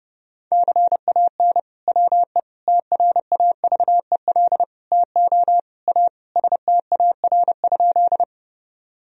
Morse code, heard as a keyed tone, about 30 wpm, 700 Hz. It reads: CAN WE TRAVEL TO A STAR?
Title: Sentences from Top 1000 Words - Review Part 2 - 30wpm